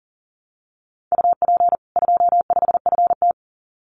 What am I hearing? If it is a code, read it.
Morse code, heard as a keyed tone, about 40 wpm, 700 Hz. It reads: UP 25FT
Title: Common QSO Phrases 40wpm